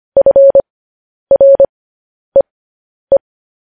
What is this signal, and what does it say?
Morse code, heard as a keyed tone, about 25 wpm, spaced as for 10 wpm, 550 Hz. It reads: FREE
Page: news-headlines-in-morse-code-at-10-wpm